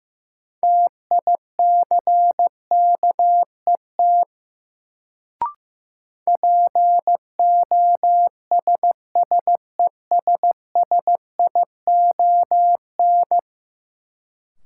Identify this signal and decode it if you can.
Morse code, heard as a keyed tone, about 15 wpm, 700 Hz. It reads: TICKET POSSESSION
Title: Sentences from Top 1400 Words - New Word Drill Alt - 15wpm